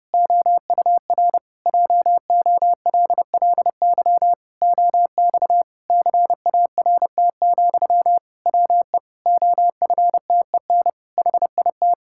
Morse code, heard as a keyed tone, about 30 wpm, 700 Hz. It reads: OUR JOLLY OX CART, WE OFTEN HIT
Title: Pangram Challenge v2 - 30wpm